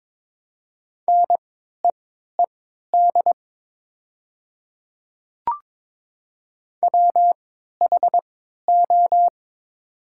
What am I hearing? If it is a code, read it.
Morse code, heard as a keyed tone, about 21 wpm, spaced as for 12 wpm, 700 Hz. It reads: NEED WHO
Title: 3x Character Spacing - Top 300 Words - 22wpm